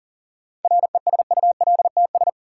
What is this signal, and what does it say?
Morse code, heard as a keyed tone, about 40 wpm, 700 Hz. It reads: RESULTS
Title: Sets of 4 Words - New Word Drill - 40wpm